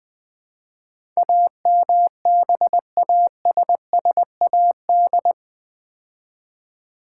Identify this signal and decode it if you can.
Morse code, heard as a keyed tone, about 20 wpm, 700 Hz. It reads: AMBASSAD
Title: Build A Word - 4 Letters At A Time - 10-Letter Words - 20wpm